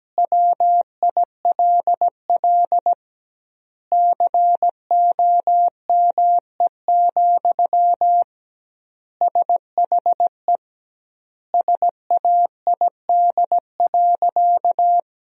Morse code, heard as a keyed tone, about 17 wpm, 700 Hz. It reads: WILL COME, SHE SAID.
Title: 2x Word-Spacing - Sentences from Top 100 Words - 17wpm